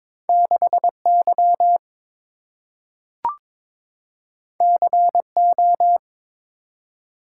Morse code, heard as a keyed tone, about 23 wpm, 700 Hz. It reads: 6Y CO
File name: practice / 22wpm / Two Letter-Number Combinations - 22wpm